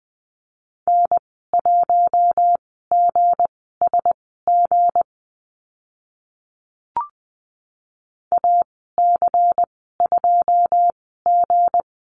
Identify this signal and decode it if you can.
Morse code, heard as a keyed tone, about 20 wpm, spaced as for 15 wpm, 700 Hz. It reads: N1GSG AC2G
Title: 161 Lesson 26g - Letter g - Call Signs 20wpm